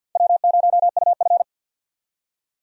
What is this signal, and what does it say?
Morse code, heard as a keyed tone, about 48 wpm, 700 Hz. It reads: W0UF
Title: K1USN SST QSO Practice - 50wpm